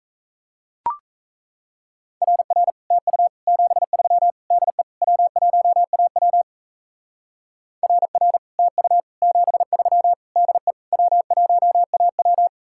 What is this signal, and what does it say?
Morse code, heard as a keyed tone, about 34 wpm, 700 Hz. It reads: RR TU 73 DE W1AW RR TU 73 DE W1AW
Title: Speed-Racing Common QSO Phrases Encore - 28wpm